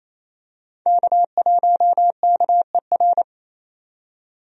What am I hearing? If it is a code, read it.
Morse code, heard as a keyed tone, about 28 wpm, 700 Hz. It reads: K1KER
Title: USA Call Signs - 5 Characters - 28wpm